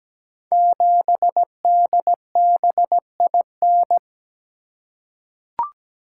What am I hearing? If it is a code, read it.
Morse code, heard as a keyed tone, about 17 wpm, 700 Hz. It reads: 7DBIN
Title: Code Groups - 5 Letter:Numbers - 17wpm